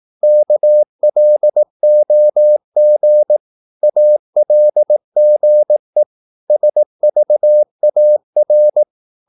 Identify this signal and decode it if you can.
Morse code, heard as a keyed tone, about 18 wpm, 600 Hz. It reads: KLOG ALGE SVAR
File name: Korte ord DK 18wpm